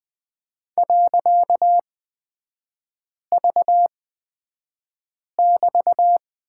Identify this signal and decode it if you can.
Morse code, heard as a keyed tone, about 20 wpm, spaced as for 5 wpm, 700 Hz. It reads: .V<BT>
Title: 255 Lesson 41e - Period - Three Characters 20wpm